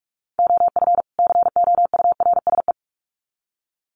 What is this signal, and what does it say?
Morse code, heard as a keyed tone, about 44 wpm, 700 Hz. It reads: OF COURSE
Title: Sets of 2 Words 45wpm